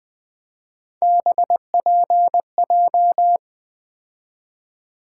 Morse code, heard as a keyed tone, about 20 wpm, 700 Hz. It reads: BPJ
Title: Three Letter-Number Combinations - 20wpm